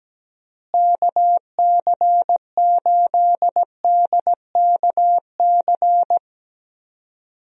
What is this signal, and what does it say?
Morse code, heard as a keyed tone, about 17 wpm, 700 Hz. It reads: KC8DKC
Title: USA Call Signs - 6 Characters - Repeated 3x - 17wpm